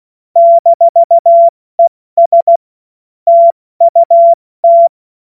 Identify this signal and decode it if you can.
Morse code, heard as a keyed tone, about 16 wpm, 680 Hz. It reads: -ES TUT
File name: Morsekurs / Witze / Witz 10. Mai 2022 70 F80